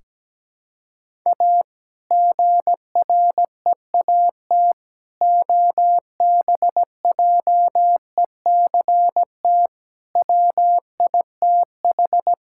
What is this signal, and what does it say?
Morse code, heard as a keyed tone, about 17 wpm, 700 Hz. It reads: A GREAT OBJECT WITH
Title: Sentences from Top 1600 Words 17wpm